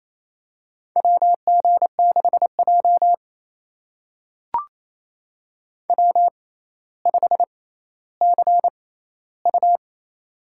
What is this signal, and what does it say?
Morse code, heard as a keyed tone, about 27 wpm, spaced as for 14 wpm, 700 Hz. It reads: WG6J W5CU
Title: USA Call Signs - 4 Characters - 6x Character Spacing - 28wpm